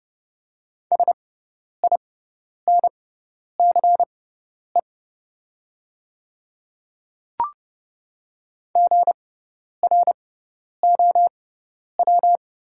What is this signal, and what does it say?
Morse code, heard as a keyed tone, about 30 wpm, spaced as for 10 wpm, 690 Hz. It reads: SINCE GROW
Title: 6x Character Spacing - Top 300 Words - 30wpm